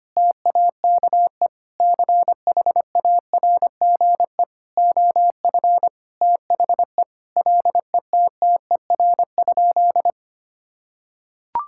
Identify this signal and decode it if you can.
Morse code, heard as a keyed tone, about 25 wpm, 700 Hz. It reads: TAKE CHARGE OF THE LETTER?
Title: Sentences from Top 1000 Words - Review Part 6 - 25wpm